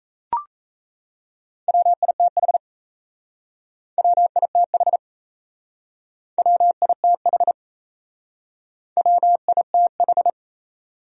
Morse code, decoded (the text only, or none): WITH WITH WITH WITH